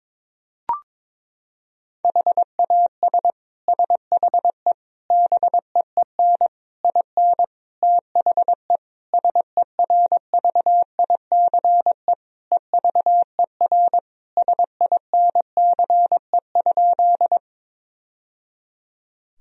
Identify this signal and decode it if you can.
Morse code, heard as a keyed tone, about 22 wpm, 700 Hz. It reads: HAS SHE BEEN IN THE SERVICE EVER SINCE?
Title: Sentences from Top 1000 Words - Review Part 4 - 22wpm